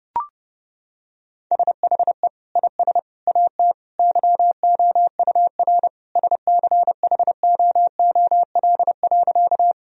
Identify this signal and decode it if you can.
Morse code, heard as a keyed tone, about 30 wpm, 700 Hz. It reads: SHE IS AT YOUR SCHOOL.
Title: Sentences from Top 1000 Words - Review Part 1 - 30wpm